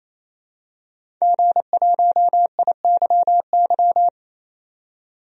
Morse code, heard as a keyed tone, about 27 wpm, 700 Hz. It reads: G1IYY
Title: DXCC-100 Call Signs - 5 Characters - 28wpm